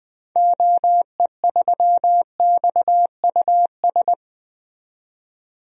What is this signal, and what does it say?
Morse code, heard as a keyed tone, about 20 wpm, 700 Hz. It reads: OE3XUS
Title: DXCC-100 Call Signs - 6 Characters - 20wpm